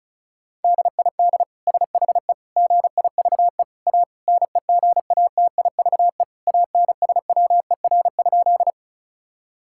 Morse code, heard as a keyed tone, about 36 wpm, 700 Hz. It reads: DID SHE GIVE A NEGATIVE ANSWER?
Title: Sentences from Top 1000 Words - Review Part 6 - 35wpm